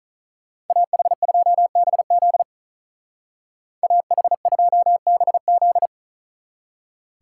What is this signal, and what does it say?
Morse code, heard as a keyed tone, about 43 wpm, 700 Hz. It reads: AH2BZ AH2BZ
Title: Speed Racing - USA 2x2 Call Signs 35wpm